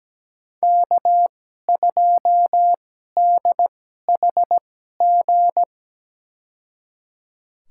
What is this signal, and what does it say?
Morse code, heard as a keyed tone, about 17 wpm, spaced as for 12 wpm, 700 Hz. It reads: K2DHG
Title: USA Call Signs - 5 Characters - 2x Character Spacing - 17wpm